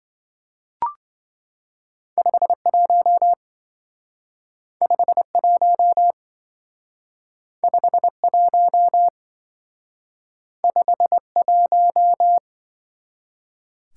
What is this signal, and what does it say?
Morse code, heard as a keyed tone, about 26 wpm, 700 Hz. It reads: 51 51 51 51